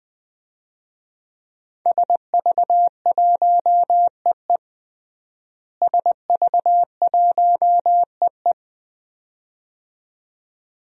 Morse code, heard as a keyed tone, about 21 wpm, 700 Hz. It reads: SV1EE SV1EE
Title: DXCC-100 Call Signs - 5 Characters - Repeated 2x - 20wpm